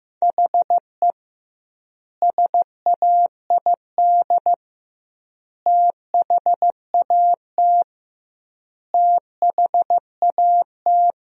Morse code, heard as a keyed tone, about 15 wpm, 700 Hz. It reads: HE SAID THAT THAT
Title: 2x Word-Spacing - Sentences from Top 200 Words - 15wpm